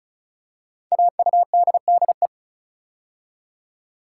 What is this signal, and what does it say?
Morse code, heard as a keyed tone, about 36 wpm, 700 Hz. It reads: AUDDE
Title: Code Groups - 5 Letters - 35wpm